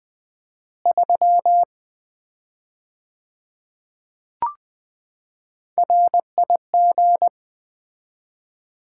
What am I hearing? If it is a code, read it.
Morse code, heard as a keyed tone, about 20 wpm, 700 Hz. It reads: 3 RIG
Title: Common QSO Phrases - 4x Word-Spacing - 20wpm